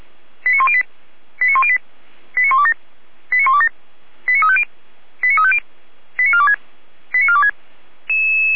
Five-tone selective call (ZVEI), heard as quick runs of five tones, tones of about 70 ms. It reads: 89108 89108 89127 89127 89350 89350 89426 89426